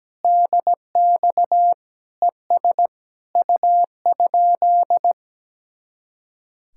Morse code, heard as a keyed tone, about 17 wpm, 700 Hz. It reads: DX ES U?
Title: Speed-Racing Common QSO Phrases - 17wpm